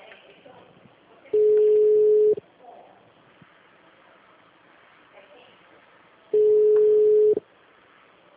Ringback tone (Europe, 425 Hz).
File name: out- -2010- - - .38040